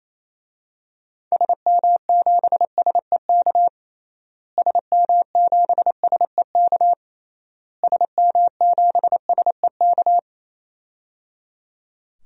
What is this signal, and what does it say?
Morse code, heard as a keyed tone, about 28 wpm, 700 Hz. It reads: SM7SEK SM7SEK SM7SEK
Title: DXCC-100 Call Signs - 6 Characters - Repeated 3x - 28wpm